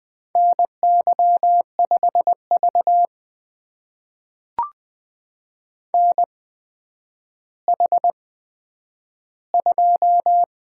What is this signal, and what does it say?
Morse code, heard as a keyed tone, about 20 wpm, spaced as for 10 wpm, 700 Hz. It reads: NY5V NH2
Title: USA Call Signs - 8x Character Spacing - 20wpm